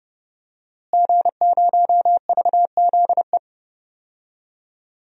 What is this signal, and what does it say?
Morse code, heard as a keyed tone, about 30 wpm, 700 Hz. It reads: G0VZE
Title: DXCC-100 Call Signs - 5 Characters - Repeated 3x - 30wpm